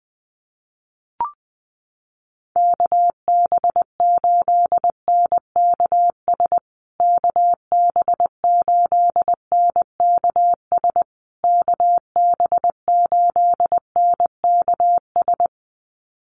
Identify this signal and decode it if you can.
Morse code, heard as a keyed tone, about 20 wpm, 700 Hz. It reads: KB8NKS KB8NKS KB8NKS
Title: USA Call Signs Repeated 3x 20wpm